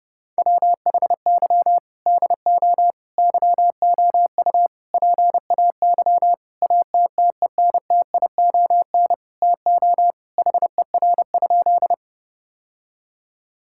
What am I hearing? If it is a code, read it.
Morse code, heard as a keyed tone, about 30 wpm, 700 Hz. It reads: WHY DO YOU PAY ATTENTION TO HER?
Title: Sentences from Top 1000 Words - Review Part 6 - 30wpm